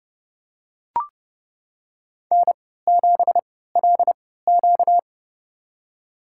Morse code, heard as a keyed tone, about 32 wpm, spaced as for 17 wpm, 700 Hz. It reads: N7LQ
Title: USA Call Signs - 4 Characters - 3x Character Spacing - 30wpm